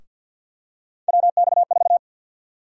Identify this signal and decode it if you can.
Morse code, heard as a keyed tone, about 52 wpm, 700 Hz. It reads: WX4
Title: USA Call Sign Prefixes - Repeated 2x - 50wpm